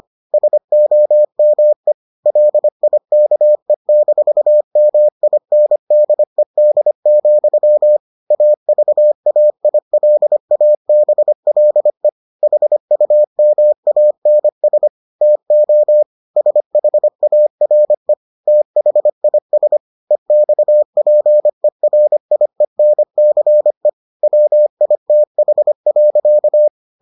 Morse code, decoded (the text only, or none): SOME LIKE-MINDED, AVAILABLE HUMANS TO SHARE THIS EXPERIENCE WITH.